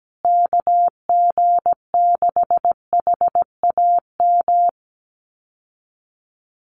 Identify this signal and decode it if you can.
Morse code, heard as a keyed tone, about 17 wpm, 700 Hz. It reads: KG6HAM
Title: USA Call Signs Repeated 2x 17wpm